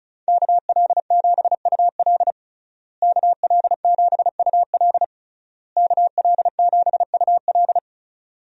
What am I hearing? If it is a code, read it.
Morse code, heard as a keyed tone, about 36 wpm, 700 Hz. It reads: KL7UL KL7UL KL7UL
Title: USA Call Signs - 5 Characters - Repeated 3x - 35wpm